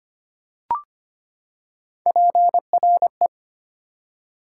Morse code, heard as a keyed tone, about 25 wpm, 700 Hz. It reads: PRE
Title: Prefixes and Suffixes 25wpm